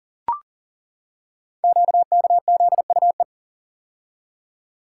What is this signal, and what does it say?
Morse code, heard as a keyed tone, about 39 wpm, 700 Hz. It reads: QKZUE
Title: Code Groups - 5 Letters - 40wpm